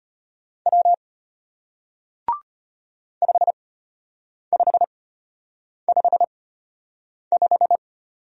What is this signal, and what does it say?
Morse code, heard as a keyed tone, about 33 wpm, spaced as for 8 wpm, 700 Hz. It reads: W 5555